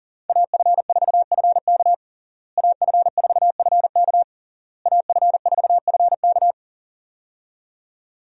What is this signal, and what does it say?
Morse code, heard as a keyed tone, about 37 wpm, 700 Hz. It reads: AF4FK AF4FK AF4FK
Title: USA Call Signs - 5 Characters - Repeated 3x - 40wpm